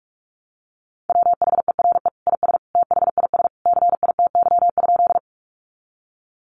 Morse code, heard as a keyed tone, about 44 wpm, 700 Hz. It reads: WHERE IS THIS CITY?
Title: Sentences from Top 300 Words 45wpm